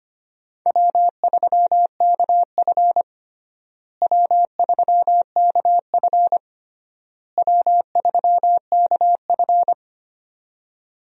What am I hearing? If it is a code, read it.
Morse code, heard as a keyed tone, about 25 wpm, 700 Hz. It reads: W3KF W3KF W3KF
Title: USA Call Signs - 4 Characters - Repeated 3x - 25wpm